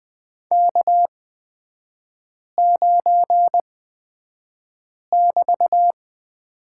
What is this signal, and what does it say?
Morse code, heard as a keyed tone, about 20 wpm, spaced as for 5 wpm, 690 Hz. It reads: K9<BT>
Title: 236 Lesson 38e - Letter q - Three Characters 20wpm